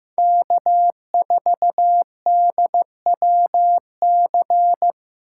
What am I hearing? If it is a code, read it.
Morse code, heard as a keyed tone, about 15 wpm, 700 Hz. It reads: K4DWC
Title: USA Call Signs - 5 Characters - 8x Character Spacing - 15wpm